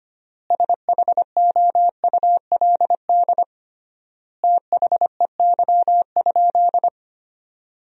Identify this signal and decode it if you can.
Morse code, heard as a keyed tone, about 25 wpm, 700 Hz. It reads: SHOULD THEY?
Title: 3x Word-Spacing - Sentences from Top 300 Words - 25wpm